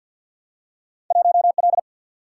Morse code, heard as a keyed tone, about 48 wpm, 700 Hz. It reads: 1L
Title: Single-Letter Single-Number 50wpm